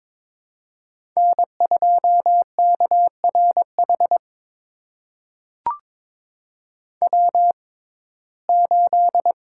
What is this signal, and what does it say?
Morse code, heard as a keyed tone, about 22 wpm, spaced as for 16 wpm, 700 Hz. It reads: N2KRH W8
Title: USA Call Signs - 6x Character Spacing - 22wpm